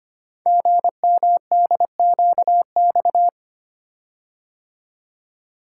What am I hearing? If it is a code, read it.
Morse code, heard as a keyed tone, about 25 wpm, 700 Hz. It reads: GMDQX
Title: Code Groups - 5 Letters - 25wpm